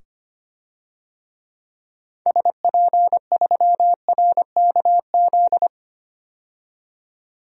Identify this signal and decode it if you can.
Morse code, heard as a keyed tone, about 25 wpm, 700 Hz. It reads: SP3RKZ
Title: DXCC-100 Call Signs - 6 Characters - 25wpm